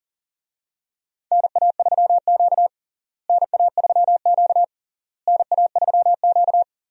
Morse code, heard as a keyed tone, about 37 wpm, 700 Hz. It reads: NA3Q NA3Q NA3Q
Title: USA Call Signs - 4 Characters - Repeated 3x - 40wpm